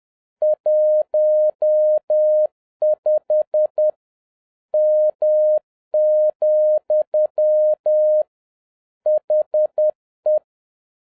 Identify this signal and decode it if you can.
Morse code, heard as a keyed tone, about 10 wpm, 610 Hz. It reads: 15 M, HE